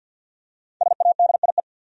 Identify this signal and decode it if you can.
Morse code, heard as a keyed tone, about 51 wpm, 700 Hz. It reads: SADIE
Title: USA Common Names - 50wpm